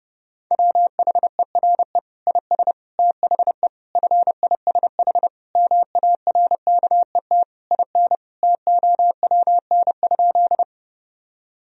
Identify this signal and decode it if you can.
Morse code, heard as a keyed tone, about 31 wpm, 700 Hz. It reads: WHERE IS THE FISH MARKET IN TOWN?
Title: Sentences from Top 1000 Words - Review Part 3 - 30wpm